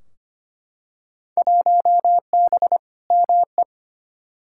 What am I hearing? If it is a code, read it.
Morse code, heard as a keyed tone, about 25 wpm, 700 Hz. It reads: 1B ME
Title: Field Day Exchanges - 25wpm